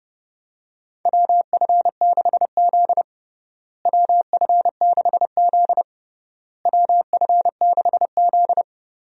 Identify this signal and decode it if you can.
Morse code, heard as a keyed tone, about 32 wpm, 700 Hz. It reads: WF6Z WF6Z WF6Z
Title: USA Call Signs - 4 Characters - Repeated 3x - 30wpm